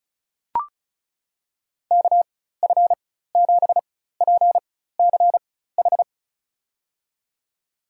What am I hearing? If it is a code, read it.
Morse code, heard as a keyed tone, about 36 wpm, spaced as for 16 wpm, 700 Hz. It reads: KF7PCH
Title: USA Call Signs - 6 Characters - 4x Character Spacing - 35wpm